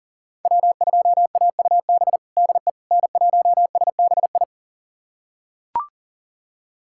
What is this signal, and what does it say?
Morse code, heard as a keyed tone, about 37 wpm, 700 Hz. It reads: W2AUB DE N1SBI
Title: USA Callsign DE Callsign 40wpm